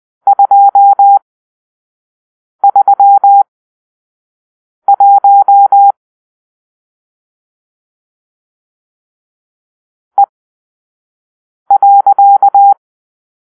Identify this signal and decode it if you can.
Morse code, heard as a keyed tone, about 20 wpm, spaced as for 5 wpm, 800 Hz. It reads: 231 E.
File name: arrl / Grp-6 Drill-3